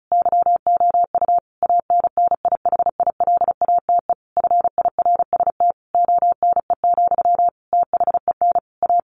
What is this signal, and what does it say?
Morse code, heard as a keyed tone, about 34 wpm, 700 Hz. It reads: YOU ANNIHILATE FIRST ONE, THEN A
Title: A Sound Of Thunder 35wpm